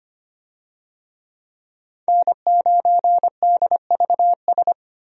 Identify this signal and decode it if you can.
Morse code, heard as a keyed tone, about 25 wpm, 700 Hz. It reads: N9DVS
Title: USA Call Signs - 5 Characters - 25wpm